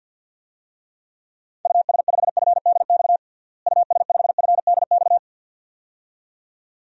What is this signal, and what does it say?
Morse code, heard as a keyed tone, about 51 wpm, 700 Hz. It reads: US5FDX US5FDX
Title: DXCC-100 Call Signs - 6 Characters - Repeated 2x - 50wpm